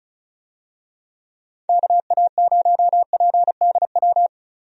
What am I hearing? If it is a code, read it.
Morse code, heard as a keyed tone, about 36 wpm, 700 Hz. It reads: KA0PDW
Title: USA Call Signs - 6 Characters - Repeated 3x - 35wpm